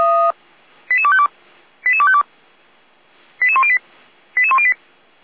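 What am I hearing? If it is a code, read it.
Five-tone selective call (ZVEI), heard as quick runs of five tones, tones of about 70 ms. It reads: 88252 88252 88108 88108
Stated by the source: ZVEI five-tone selective call 88102 received with an SDR.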